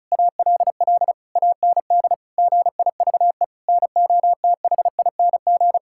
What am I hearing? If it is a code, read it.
Morse code, heard as a keyed tone, about 36 wpm, 700 Hz. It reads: ALL AND GIVE NOTHING
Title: Sentences from Top 1000 Words - Review Part 6 - 35wpm